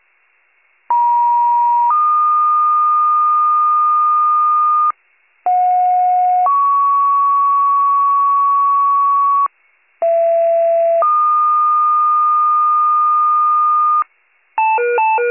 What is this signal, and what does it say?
Two-tone page, heard as a one-second tone, then a three-second tone, about 950 then 1220 Hz, about 730 then 1120 Hz, about 670 then 1220 Hz. It is heard from 0.9 to 14.0 s.